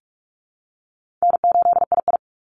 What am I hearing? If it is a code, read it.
Morse code, heard as a keyed tone, about 44 wpm, 700 Hz. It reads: N8II
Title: Easy Call Signs Repeated 2x - 45wpm